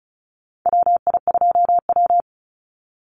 Morse code, heard as a keyed tone, about 34 wpm, 700 Hz. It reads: WI2W
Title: NA SOTA Chaser Callsigns 35wpm